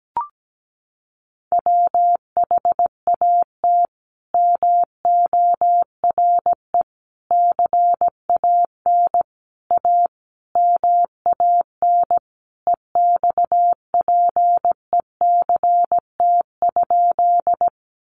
Morse code, heard as a keyed tone, about 17 wpm, 700 Hz. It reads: WHAT MORE CAN A MAN EXPECT?
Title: Sentences from Top 1100 Words 17wpm